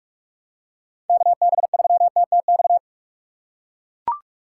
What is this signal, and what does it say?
Morse code, heard as a keyed tone, about 42 wpm, 700 Hz. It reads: KB3TTX
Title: USA Call Signs - 6 Characters - 6x Character Spacing - 45wpm